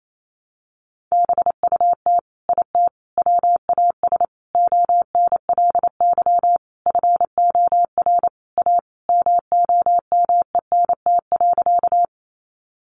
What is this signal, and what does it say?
Morse code, heard as a keyed tone, about 28 wpm, 700 Hz. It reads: BUT IT WAS ONLY FOR A MOMENT.
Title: Sentences from Top 1000 Words 28wpm